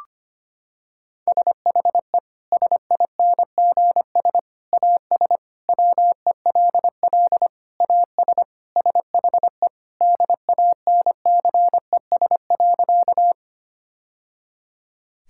Morse code, heard as a keyed tone, about 25 wpm, 700 Hz. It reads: SHE SINGS AS WELL AS SHE DANCES.
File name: practice / 25wpm / Sentences with 3-Word Phrases - 25wpm